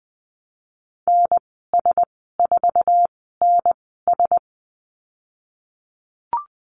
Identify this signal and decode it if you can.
Morse code, heard as a keyed tone, about 20 wpm, spaced as for 15 wpm, 700 Hz. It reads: NS4NS
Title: 140 Lesson 23g - Letter f - Call Signs 20wpm